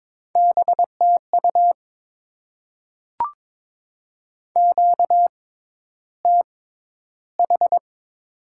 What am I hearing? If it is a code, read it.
Morse code, heard as a keyed tone, about 22 wpm, spaced as for 11 wpm, 700 Hz. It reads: BTU QTH
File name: Top 20 QSO Elements - 6x Character Spacing - 22wpm